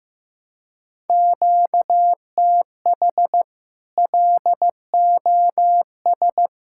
Morse code, heard as a keyed tone, about 15 wpm, 700 Hz. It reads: QTH LOS
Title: Practice QSO - Exchange 1 - Beginning - 15wpm